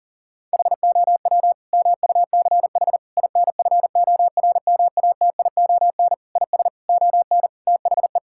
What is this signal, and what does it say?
Morse code, heard as a keyed tone, about 40 wpm, 700 Hz. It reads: HOW MUCH INFORMATION IS ON THE
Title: Sentences from Top 1000 Words - Review Part 6 - 40wpm